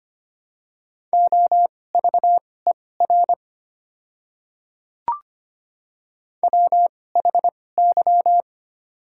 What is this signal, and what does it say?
Morse code, heard as a keyed tone, about 25 wpm, spaced as for 18 wpm, 700 Hz. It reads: OVER WHY
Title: 2x Character Spacing - Top 200 Words - 25wpm